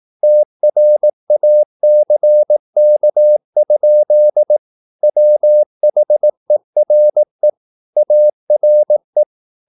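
Morse code, heard as a keyed tone, about 18 wpm, 600 Hz. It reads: TRACK? WHERE ARE
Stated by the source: Morse practice at 18 wpm, English text, from CW med Gnister.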